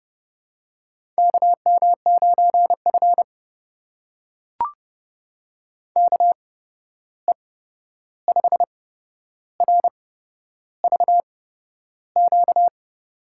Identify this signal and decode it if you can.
Morse code, heard as a keyed tone, about 30 wpm, spaced as for 11 wpm, 700 Hz. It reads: KM9F KE5RVQ
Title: USA Call Signs - 8x Character Spacing - 30wpm